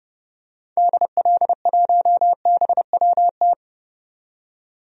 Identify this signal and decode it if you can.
Morse code, heard as a keyed tone, about 30 wpm, 700 Hz. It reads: DL1BWT
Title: DXCC-100 Call Signs - 6 Characters - 30wpm